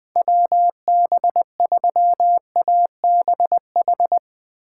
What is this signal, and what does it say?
Morse code, heard as a keyed tone, about 21 wpm, 700 Hz. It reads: WB3ABH
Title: USA Call Signs - 6 Characters - 2x Character Spacing - 20wpm